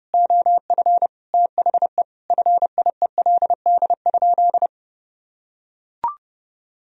Morse code, heard as a keyed tone, about 32 wpm, 700 Hz. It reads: OF THE FIELD?
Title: Sentences from Top 1000 Words - Review Part 2 - 30wpm